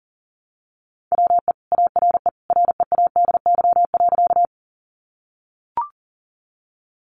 Morse code, heard as a keyed tone, about 40 wpm, 700 Hz. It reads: WE ARE READY.
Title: Sentences from Top 400 Words 40wpm